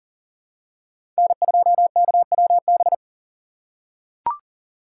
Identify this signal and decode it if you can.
Morse code, heard as a keyed tone, about 40 wpm, 700 Hz. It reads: N2KWB
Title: USA Call Signs - 5 Characters - 3x Character Spacing - 40wpm